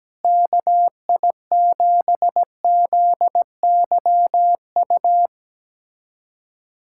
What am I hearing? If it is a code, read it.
Morse code, heard as a keyed tone, about 17 wpm, 700 Hz. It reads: KI7ZYU
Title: USA Callsign DE Callsign - 6 Characters - 17wpm